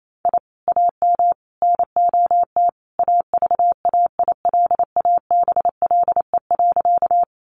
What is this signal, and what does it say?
Morse code, heard as a keyed tone, about 28 wpm, 700 Hz. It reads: I AM NOT AVAILABLE.
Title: Sentences from Top 600 Words 28wpm